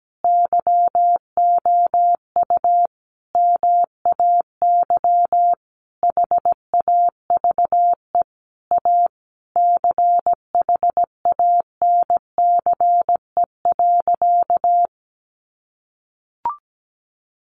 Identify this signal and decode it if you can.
Morse code, heard as a keyed tone, about 17 wpm, 700 Hz. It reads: YOU MAY HAVE A CHANCE.
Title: Sentences from Top 1400 Words 17wpm